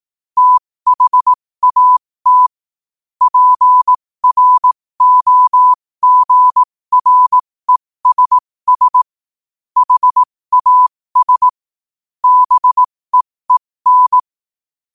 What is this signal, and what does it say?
Morse code, heard as a keyed tone, about 18 wpm, spaced as for 15 wpm, 1000 Hz. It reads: THAT PROGRESS HAS BEEN
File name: qotdfiles / 15wpm